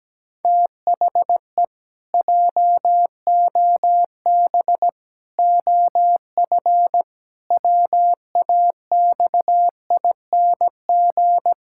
Morse code, heard as a keyed tone, about 17 wpm, 700 Hz. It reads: THE JOB OF WAXING
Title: Pangram Challenge v2 - 17wpm